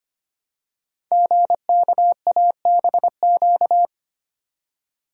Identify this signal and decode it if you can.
Morse code, heard as a keyed tone, about 25 wpm, 700 Hz. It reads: GKABQ
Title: Code Groups - 5 Letters - 25wpm